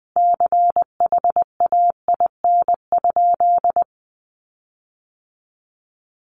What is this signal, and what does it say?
Morse code, heard as a keyed tone, about 20 wpm, 700 Hz. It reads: CHAIN?
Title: Sentences from Top 1500 Words 20wpm